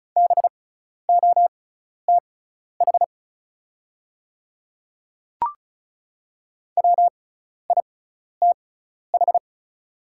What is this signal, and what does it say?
Morse code, heard as a keyed tone, about 36 wpm, spaced as for 12 wpm, 690 Hz. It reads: BOTH WITH
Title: 6x Character Spacing - Top 500 Words - 35wpm